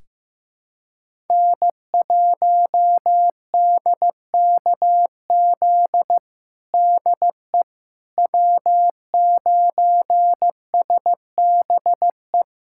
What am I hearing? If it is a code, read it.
Morse code, heard as a keyed tone, about 15 wpm, 700 Hz. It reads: N1DKZ DE W9SBE
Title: USA Callsign DE Callsign - 5 Characters - 15wpm